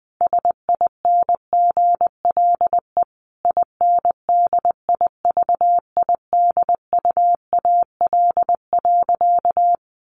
Morse code, heard as a keyed tone, about 19 wpm, 700 Hz. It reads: SINGLE INDIVIDUAL.
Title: Sentences from Top 1500 Words 20wpm